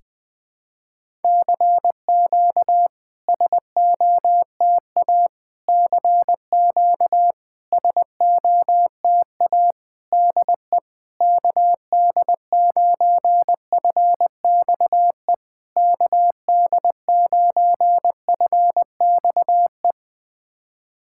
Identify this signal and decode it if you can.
Morse code, heard as a keyed tone, about 20 wpm, 690 Hz. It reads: CQ SOTA CQ SOTA DE KD9FXE KD9FXE
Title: SOTA QSO Practice 20wpm